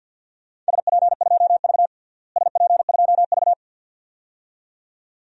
Morse code, heard as a keyed tone, about 47 wpm, 700 Hz. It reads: SP2V SP2V
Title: DXCC-100 Call Signs - 4 Characters - Repeated 2x - 50wpm